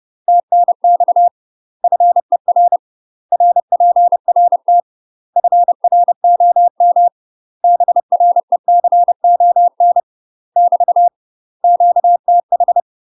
Morse code, heard as a keyed tone, about 30 wpm, 700 Hz. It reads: TNX FER RPRT FROM BRECON <BT> QTH
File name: UK MASTER QSOs TRAINING / 30WPM / UK MASTER QSOs 0009